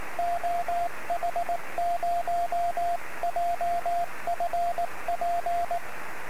Morse code, heard as a keyed tone, about 18 wpm, 690 Hz, with radio noise. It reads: OH0JFP
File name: sm7lcb / 01-oh0jfp-23cm